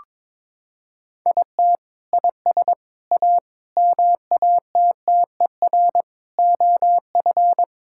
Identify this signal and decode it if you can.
Morse code, heard as a keyed tone, about 22 wpm, 700 Hz. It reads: IT IS A MATTER OF
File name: practice / 22wpm / Sentences with 3-Word Phrases - 22wpm